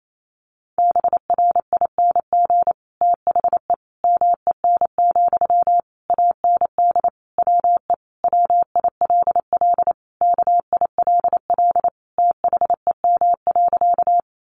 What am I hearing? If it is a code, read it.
Morse code, heard as a keyed tone, about 28 wpm, 700 Hz. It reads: BRING THE MEN, AND WE WILL KILL THEM.
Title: Sentences from Top 1400 Words 28wpm